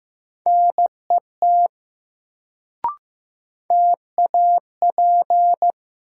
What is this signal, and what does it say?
Morse code, heard as a keyed tone, about 15 wpm, 700 Hz. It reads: NET TAP
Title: Sentences from Top 1400 Words - New Word Drill Alt - 15wpm